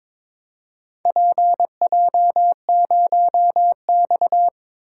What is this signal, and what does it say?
Morse code, heard as a keyed tone, about 22 wpm, 700 Hz. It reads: PJ0X
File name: practice / 22wpm / Four Letter-Number Combinations - 22wpm